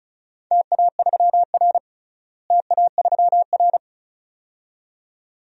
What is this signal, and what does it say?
Morse code, heard as a keyed tone, about 36 wpm, 690 Hz. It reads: TA3R TA3R
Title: DXCC-100 Call Signs - 4 Characters - Repeated 2x - 35wpm